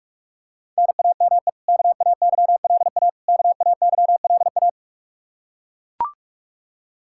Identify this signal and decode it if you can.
Morse code, heard as a keyed tone, about 41 wpm, 700 Hz. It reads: NAME KAYLA KAYLA
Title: Practice QSO - Exchange 1 - Beginning - 45wpm